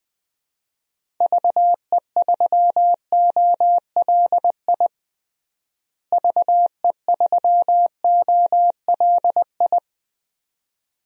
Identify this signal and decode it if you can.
Morse code, heard as a keyed tone, about 21 wpm, 700 Hz. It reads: VE3OLI VE3OLI
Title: DXCC-100 Call Signs - 6 Characters - Repeated 2x - 20wpm